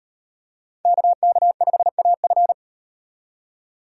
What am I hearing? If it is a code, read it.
Morse code, heard as a keyed tone, about 38 wpm, 700 Hz. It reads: KK5AF